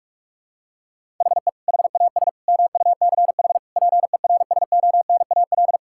Morse code, heard as a keyed tone, about 41 wpm, 700 Hz. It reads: HE HAS MUCH PERSONAL
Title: Sentences from Top 1000 Words - Review Part 5 - 45wpm